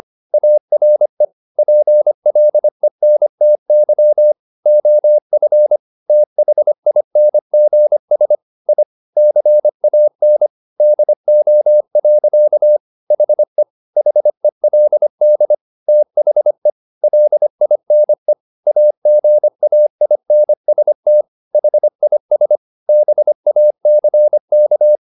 Morse code, heard as a keyed tone, about 25 wpm, 600 Hz. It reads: ARE PLENTY OF THINGS I CAN DO. HE HELD THE LINE AGAINST HIS BACK
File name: the old man and the sea part 1 25wpm